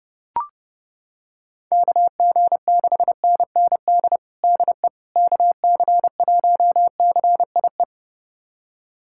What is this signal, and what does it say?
Morse code, heard as a keyed tone, about 32 wpm, 700 Hz. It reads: KG6NND DE KC1CIE
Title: USA Callsign DE Callsign - 6 Characters - 30wpm